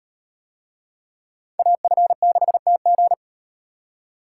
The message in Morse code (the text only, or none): AF6TG